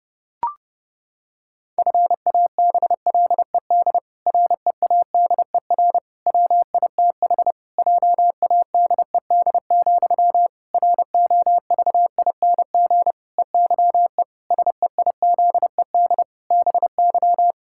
Morse code, heard as a keyed tone, about 30 wpm, 700 Hz. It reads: FABLED READER WITH JADED, ROVING EYE SEIZED BY
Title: Pangram Challenge v2 - 30wpm